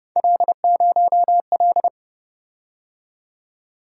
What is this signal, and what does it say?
Morse code, heard as a keyed tone, about 30 wpm, 700 Hz. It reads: L0L